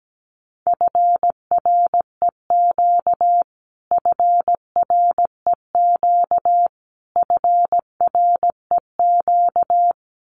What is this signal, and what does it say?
Morse code, heard as a keyed tone, about 17 wpm, 700 Hz. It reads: FREQ FREQ FREQ
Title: Abbreviations Repeated 3x - 17wpm